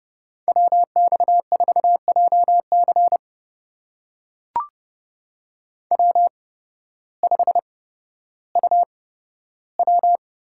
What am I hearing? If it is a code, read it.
Morse code, heard as a keyed tone, about 30 wpm, spaced as for 14 wpm, 700 Hz. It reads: WX4JC W5UW
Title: USA Call Signs - 5 Characters - 8x Character Spacing - 30wpm